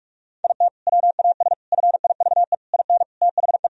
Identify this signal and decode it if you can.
Morse code, heard as a keyed tone, about 41 wpm, 700 Hz. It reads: IT WAS FIVE IN THE
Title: Sentences from Top 1000 Words - Review Part 2 - 45wpm